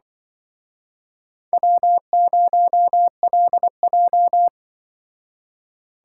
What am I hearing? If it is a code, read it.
Morse code, heard as a keyed tone, about 25 wpm, 700 Hz. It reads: W0LJ
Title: Speed Racing - USA 1x2 Call Signs 20wpm